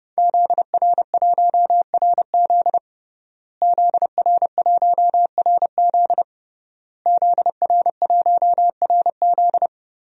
Morse code, heard as a keyed tone, about 29 wpm, 700 Hz. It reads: ZR1RZ ZR1RZ ZR1RZ
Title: DXCC-100 Call Signs - 5 Characters - Repeated 3x - 30wpm